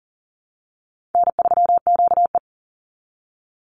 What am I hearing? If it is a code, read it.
Morse code, heard as a keyed tone, about 40 wpm, 700 Hz. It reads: N3QE
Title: Easy Call Signs Repeated 3x - 40wpm